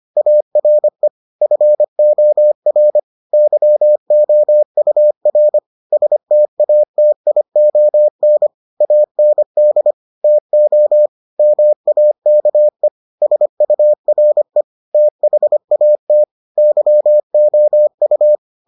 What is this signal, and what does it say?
Morse code, heard as a keyed tone, about 25 wpm, 600 Hz. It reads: ARE FOR YOUR STATION AND TO MAKE SURE THAT YOU